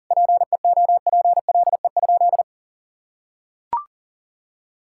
Morse code, heard as a keyed tone, about 40 wpm, 700 Hz. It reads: PEOPLE?
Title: Sentences from Top 1000 Words - Review Part 5 - 40wpm